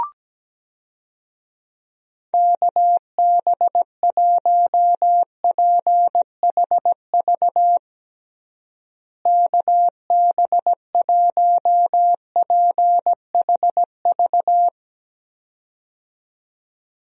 Morse code, heard as a keyed tone, about 17 wpm, 700 Hz. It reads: KB1PHV KB1PHV
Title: USA Call Signs - 6 Characters - Repeated 2x - 17wpm